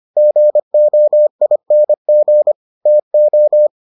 Morse code, heard as a keyed tone, about 25 wpm, 600 Hz. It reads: GOING TO